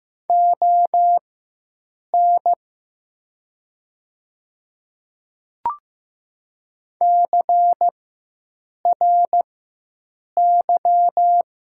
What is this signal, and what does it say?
Morse code, heard as a keyed tone, about 15 wpm, spaced as for 7 wpm, 700 Hz. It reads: ON CRY
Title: 4x Character Spacing - Top 500 Words - 15wpm